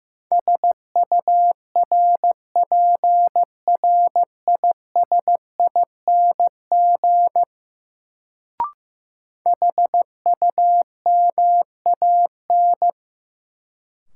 Morse code, heard as a keyed tone, about 15 wpm, 700 Hz. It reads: SURPRISING HUMAN
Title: Sets of 5 Words - New Word Drill - 15wpm